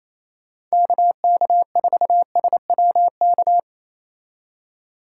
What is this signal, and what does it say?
Morse code, heard as a keyed tone, about 35 wpm, 700 Hz. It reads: KK4SWK